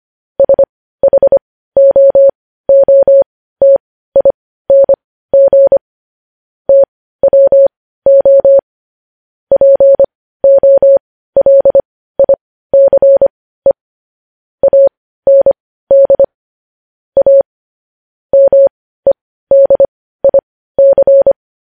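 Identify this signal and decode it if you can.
Morse code, heard as a keyed tone, about 25 wpm, spaced as for 15 wpm, 550 Hz. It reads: SHOOTING TWO POLICE AND A MEDIC
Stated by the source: Morse code news headlines sent at 15 words per minute.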